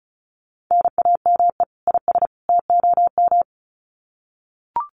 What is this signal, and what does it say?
Morse code, heard as a keyed tone, about 34 wpm, 700 Hz. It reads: NAME IS TOM
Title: Common QSO Phrases 35wpm